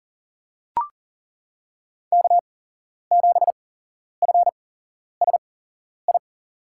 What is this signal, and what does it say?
Morse code, heard as a keyed tone, about 40 wpm, spaced as for 11 wpm, 700 Hz. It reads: K7FSI
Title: USA Call Signs - 5 Characters - 8x Character Spacing - 40wpm